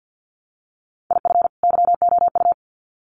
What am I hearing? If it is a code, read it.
Morse code, heard as a keyed tone, about 50 wpm, 700 Hz. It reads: IF YOU
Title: Sets of 2 Words 50wpm